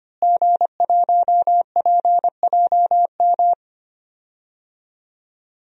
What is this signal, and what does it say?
Morse code, heard as a keyed tone, about 25 wpm, 700 Hz. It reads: G1PJM
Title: DXCC-100 Call Signs - 5 Characters - Repeated 3x - 25wpm